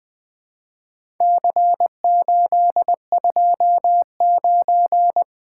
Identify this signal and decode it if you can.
Morse code, heard as a keyed tone, about 20 wpm, 700 Hz. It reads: C829
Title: Four Letter-Number Combinations - 20wpm